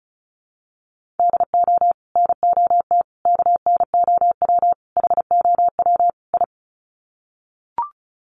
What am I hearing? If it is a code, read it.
Morse code, heard as a keyed tone, about 34 wpm, 700 Hz. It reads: DO NOT KNOW HOW I
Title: Sets of 5 Words Encore2 35wpm